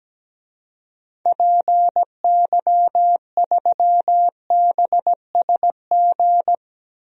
Morse code, heard as a keyed tone, about 17 wpm, 700 Hz. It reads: PY3BSG
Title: DXCC-100 Call Signs - 6 Characters - 17wpm